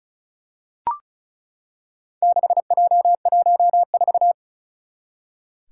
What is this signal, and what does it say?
Morse code, heard as a keyed tone, about 36 wpm, 700 Hz. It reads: 6J14
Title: Four Challenging Letter-Number Combinations 35wpm